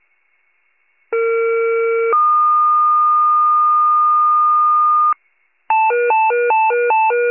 Two-tone page, heard as a one-second tone, then a three-second tone, about 460 then 1220 Hz. It is heard from 1.1 to 5.1 s.